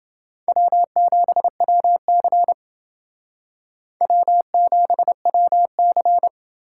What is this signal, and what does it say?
Morse code, heard as a keyed tone, about 28 wpm, 700 Hz. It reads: W7WC W7WC